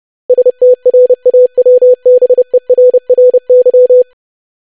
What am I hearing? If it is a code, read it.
Morse code, heard as a keyed tone, about 30 wpm, 500 Hz. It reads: STRAWBERRY
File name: lesson / web / longwords-25wpm